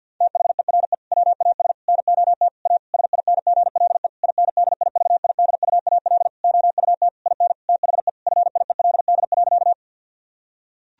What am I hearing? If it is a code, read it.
Morse code, heard as a keyed tone, about 50 wpm, 700 Hz. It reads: THERE WAS NOT A SINGLE INDIVIDUAL OUT IN THE FIELD.
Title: Sentences from Top 1000 Words - Review Part 4 - 50wpm